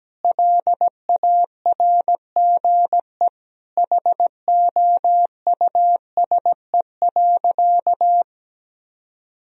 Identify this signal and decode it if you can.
Morse code, heard as a keyed tone, about 17 wpm, 700 Hz. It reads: LARGE HOUSE.
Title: Sentences from Top 1000 Words - Review Part 1 - 17wpm